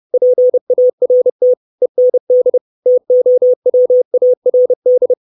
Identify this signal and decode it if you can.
Morse code, heard as a keyed tone, about 30 wpm, 500 Hz. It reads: PART END TOWARD
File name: 632 Words 30WPM